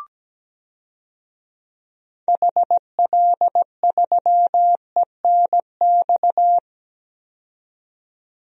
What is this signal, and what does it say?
Morse code, heard as a keyed tone, about 17 wpm, 700 Hz. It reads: HL3ENX
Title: DXCC-100 Call Signs - 6 Characters - 17wpm